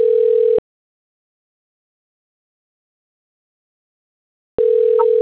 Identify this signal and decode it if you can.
Ringback tone (North America), 440 + 480 Hz.